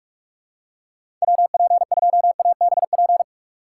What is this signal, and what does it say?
Morse code, heard as a keyed tone, about 44 wpm, 700 Hz. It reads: WP2ABP
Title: USA Call Signs - 6 Characters - 3x Character Spacing - 45wpm